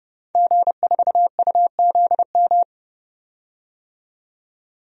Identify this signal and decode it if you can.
Morse code, heard as a keyed tone, about 30 wpm, 700 Hz. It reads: G4UZM
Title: DXCC-100 Call Signs - 5 Characters - 30wpm